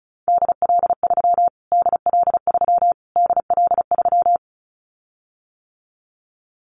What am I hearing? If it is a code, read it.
Morse code, heard as a keyed tone, about 34 wpm, 700 Hz. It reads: DL3 DL3 DL3
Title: Callsign Prefixes for Top 25 DXCC - Repeated 3x - 35wpm